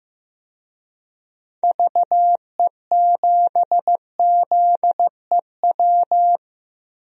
Morse code, heard as a keyed tone, about 15 wpm, 700 Hz. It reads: VE7ZEW
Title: DXCC-100 Call Signs - 6 Characters - 15wpm